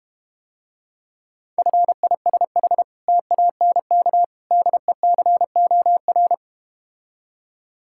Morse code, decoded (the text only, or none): FISH TANK DECOR